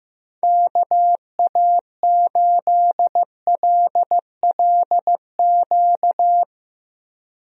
Morse code, heard as a keyed tone, about 15 wpm, 700 Hz. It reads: KA8LLQ
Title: USA Call Signs - 6 Characters - Repeated 3x - 15wpm